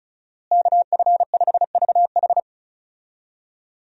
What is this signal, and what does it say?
Morse code, heard as a keyed tone, about 36 wpm, 700 Hz. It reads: KF5VH
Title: USA Call Signs - 5 Characters - Repeated 3x - 35wpm